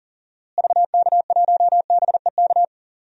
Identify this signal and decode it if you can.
Morse code, heard as a keyed tone, about 40 wpm, 700 Hz. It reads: VK1BEK
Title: DXCC-100 Call Signs - 6 Characters - Repeated 3x - 40wpm